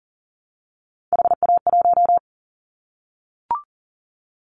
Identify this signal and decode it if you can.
Morse code, heard as a keyed tone, about 40 wpm, 700 Hz. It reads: HA1
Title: Callsign Prefixes for Top 25 DXCC - Repeated 3x - 40wpm